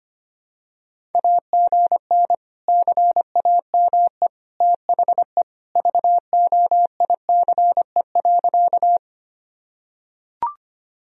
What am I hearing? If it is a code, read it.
Morse code, heard as a keyed tone, about 25 wpm, 700 Hz. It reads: AGN CAME THE VOICE.
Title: Sentences from Top 500 Words and Abbreviations 25wpm